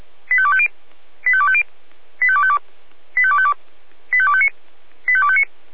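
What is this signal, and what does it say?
Five-tone selective call (ZVEI), heard as quick runs of five tones, tones of about 70 ms. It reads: 85250 85250 85252 85252 85259 85259
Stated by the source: ZVEI five-tone selective call received via SDR.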